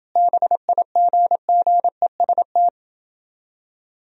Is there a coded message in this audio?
Yes: Morse code from 0.2 to 2.7 s, heard as a keyed tone, about 29 wpm, 700 Hz. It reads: BIGGEST